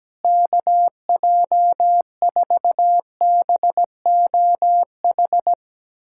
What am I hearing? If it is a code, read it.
Morse code, heard as a keyed tone, about 17 wpm, 700 Hz. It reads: KJ4BOH
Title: USA Callsign DE Callsign - 6 Characters - 17wpm